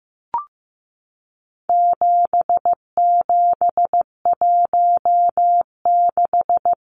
Morse code, heard as a keyed tone, about 15 wpm, 700 Hz. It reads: 7716
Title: Numbers - 4-Digit 15wpm